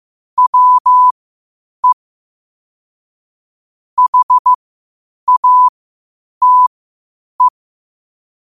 Morse code, heard as a keyed tone, about 15 wpm, spaced as for 8 wpm, 1000 Hz. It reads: WE HATE
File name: qotdfiles / 8wpm